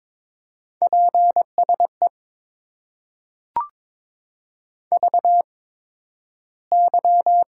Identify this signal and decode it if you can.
Morse code, heard as a keyed tone, about 22 wpm, spaced as for 12 wpm, 700 Hz. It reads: PSE VY
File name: Top 90 QSO Elements - 8x Character Spacing - 22wpm